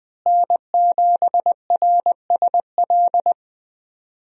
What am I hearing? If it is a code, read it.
Morse code, heard as a keyed tone, about 20 wpm, 700 Hz. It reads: N7RSL
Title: USA Call Signs - 5 Characters - Repeated 3x - 20wpm